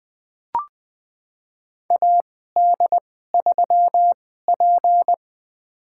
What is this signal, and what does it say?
Morse code, heard as a keyed tone, about 20 wpm, spaced as for 14 wpm, 700 Hz. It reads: AD3P
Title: USA Call Signs - 2x Character Spacing - 20wpm